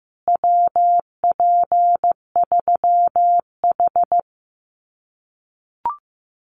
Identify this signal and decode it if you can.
Morse code, heard as a keyed tone, about 15 wpm, 700 Hz. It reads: WP3H
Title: Common QSO Phrases Encore 15wpm